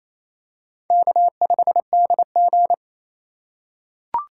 Morse code, heard as a keyed tone, about 28 wpm, 700 Hz. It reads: K5DG
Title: USA Call Signs - 4 Characters - 6x Character Spacing - 28wpm